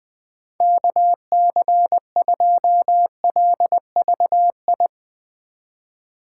Morse code, heard as a keyed tone, about 21 wpm, 700 Hz. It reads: KC2LVI
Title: USA Call Signs - 6 Characters - 4x Character Spacing - 20wpm